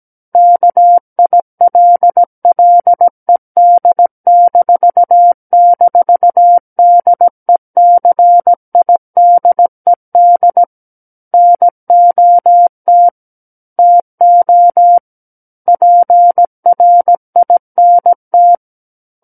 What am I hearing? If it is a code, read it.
Morse code, heard as a keyed tone, about 17 wpm, 700 Hz. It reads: KILLED--DECIDED NOT TO PRINT
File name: Book One / 08-Chapter 8 - 17 WPM